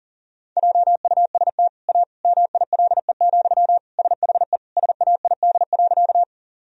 Morse code, heard as a keyed tone, about 37 wpm, 700 Hz. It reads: JUST A MILE, SHE SAID.
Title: Sentences from Top 1000 Words - Review Part 1 - 40wpm